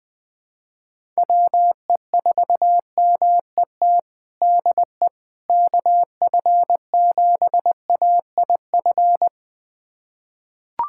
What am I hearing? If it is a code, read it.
Morse code, heard as a keyed tone, about 20 wpm, 700 Hz. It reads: WE4MET DE KF7AIF
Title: USA Callsign DE Callsign - 6 Characters - 20wpm